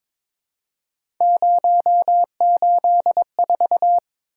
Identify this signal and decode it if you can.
Morse code, heard as a keyed tone, about 22 wpm, 700 Hz. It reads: 084
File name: practice / 22wpm / Three Letter-Number Combinations - 22wpm